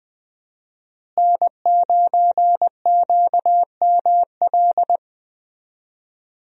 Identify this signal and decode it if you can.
Morse code, heard as a keyed tone, about 20 wpm, 700 Hz. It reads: N9QML
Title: USA Call Signs - 5 Characters - Repeated 2x - 20wpm